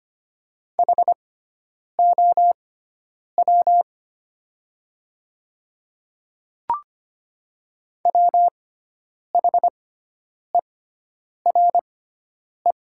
Morse code, heard as a keyed tone, about 25 wpm, spaced as for 9 wpm, 700 Hz. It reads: HOW WHERE
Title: 6x Character Spacing - Top 300 Words - 25wpm